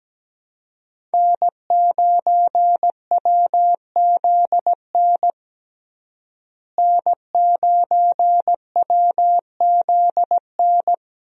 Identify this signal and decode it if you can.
Morse code, heard as a keyed tone, about 17 wpm, 700 Hz. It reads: N9WZN N9WZN
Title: USA Call Signs - 5 Characters - Repeated 3x - 17wpm